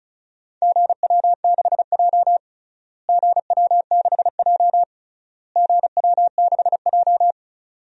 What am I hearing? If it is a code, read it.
Morse code, heard as a keyed tone, about 34 wpm, 700 Hz. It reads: GW6J GW6J GW6J
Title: DXCC-100 Call Signs - 4 Characters - Repeated 3x - 35wpm